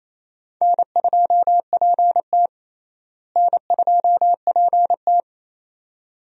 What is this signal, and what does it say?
Morse code, heard as a keyed tone, about 28 wpm, 700 Hz. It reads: N2PT N2PT
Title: USA Call Signs - 4 Characters - Repeated 2x - 28wpm